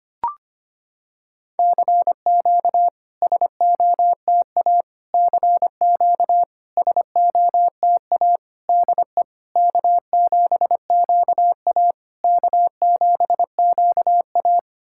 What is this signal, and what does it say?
Morse code, heard as a keyed tone, about 25 wpm, 690 Hz. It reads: CQ SOTA CQ SOTA DE K7QA K7QA
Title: SOTA QSO Practice 25wpm